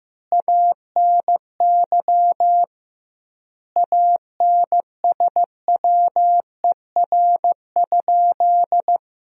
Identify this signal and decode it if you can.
Morse code, heard as a keyed tone, about 15 wpm, 700 Hz. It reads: ANY ANSWER?
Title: 2x Word-Spacing - Sentences from Top 300 Words - 15wpm